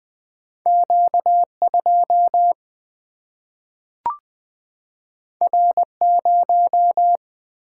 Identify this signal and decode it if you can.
Morse code, heard as a keyed tone, about 20 wpm, 700 Hz. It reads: Q 2 R 0
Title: Single-Letter Single-Number 20wpm